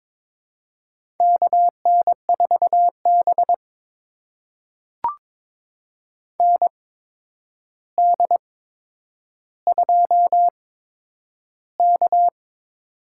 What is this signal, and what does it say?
Morse code, heard as a keyed tone, about 21 wpm, spaced as for 9 wpm, 700 Hz. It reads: KN4B ND2K
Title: USA Call Signs - 4 Characters - 8x Character Spacing - 22wpm